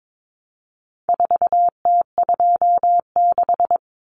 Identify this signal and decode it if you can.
Morse code, heard as a keyed tone, about 22 wpm, 700 Hz. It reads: 4T26
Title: Common Cut Numbers - 4-Digit 22wpm